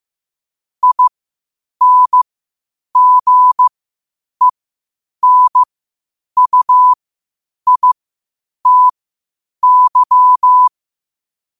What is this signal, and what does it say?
Morse code, heard as a keyed tone, about 15 wpm, spaced as for 8 wpm, 1000 Hz. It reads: INGENUITY
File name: qotdfiles / 8wpm